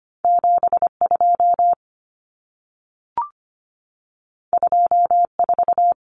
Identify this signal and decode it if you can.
Morse code, heard as a keyed tone, about 25 wpm, 700 Hz. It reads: 72 24
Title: Common Cut Numbers - 2-Digit 25wpm